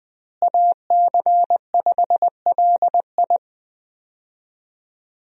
Morse code, heard as a keyed tone, about 21 wpm, 700 Hz. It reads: AC5LI
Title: USA Call Signs - 5 Characters - Repeated 2x - 20wpm